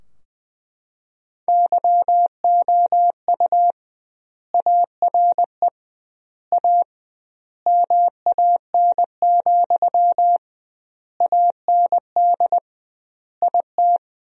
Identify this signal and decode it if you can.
Morse code, heard as a keyed tone, about 21 wpm, 700 Hz. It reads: YOU ARE A MAN, AND IT
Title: 2x Word-Spacing - Sentences from Top 200 Words - 20wpm